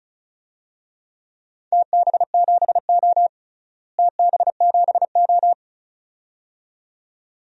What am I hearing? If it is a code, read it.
Morse code, heard as a keyed tone, about 36 wpm, 690 Hz. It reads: TB7O TB7O
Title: DXCC-100 Call Signs - 4 Characters - Repeated 2x - 35wpm